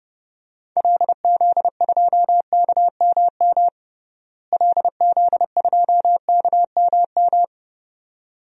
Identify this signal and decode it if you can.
Morse code, heard as a keyed tone, about 32 wpm, 700 Hz. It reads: LZ2KMM LZ2KMM
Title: DXCC-100 Call Signs - 6 Characters - Repeated 2x - 30wpm